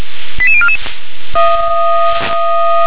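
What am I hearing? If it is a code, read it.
Five-tone selective call (ZVEI), heard as quick runs of five tones, tones of about 70 ms. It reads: 88044
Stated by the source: ZVEI five-tone selective call received with an SDR.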